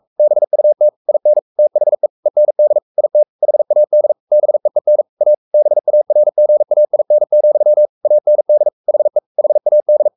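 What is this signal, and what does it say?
Morse code, heard as a keyed tone, about 44 wpm, 600 Hz. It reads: BUT IN THE END IT HAD BEEN A BARGAIN, AND HE HAD
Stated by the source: Morse code at 43 wpm sending Ken Follett's novel Never in English, from CW med Gnister.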